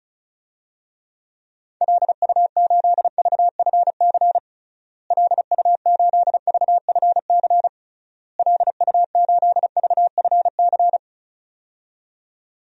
Morse code, heard as a keyed tone, about 36 wpm, 700 Hz. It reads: LU8VFC LU8VFC LU8VFC
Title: DXCC-100 Call Signs - 6 Characters - Repeated 3x - 35wpm